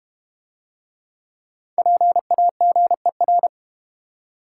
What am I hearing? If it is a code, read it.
Morse code, heard as a keyed tone, about 31 wpm, 700 Hz. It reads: PAGER
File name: practice / 25wpm / Difficult Words - Speed-Racing with Hints - 25wpm